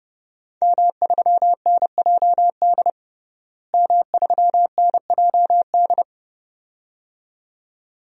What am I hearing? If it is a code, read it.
Morse code, heard as a keyed tone, about 30 wpm, 700 Hz. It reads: M3NJD M3NJD
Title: DXCC-100 Call Signs - 5 Characters - Repeated 2x - 30wpm